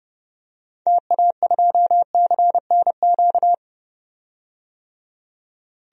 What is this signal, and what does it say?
Morse code, heard as a keyed tone, about 30 wpm, 700 Hz. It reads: TA2CNQ
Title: DXCC-100 Call Signs - 6 Characters - 30wpm